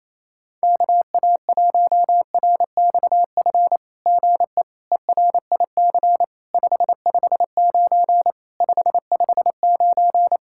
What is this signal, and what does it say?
Morse code, heard as a keyed tone, about 28 wpm, 700 Hz. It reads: KA1RXF GE ERIC 559 559
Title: SOTA QSO Practice 28wpm